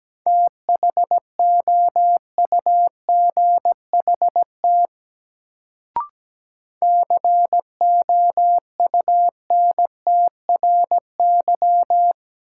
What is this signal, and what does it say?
Morse code, heard as a keyed tone, about 17 wpm, 700 Hz. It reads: THOUGHT COUNTRY
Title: Sets of 2 Words - New Word Drill - 17wpm